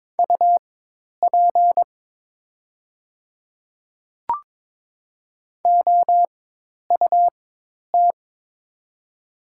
Morse code, heard as a keyed tone, about 22 wpm, spaced as for 10 wpm, 690 Hz. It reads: UP OUT
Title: 4x Character Spacing - Top 200 Words - 22wpm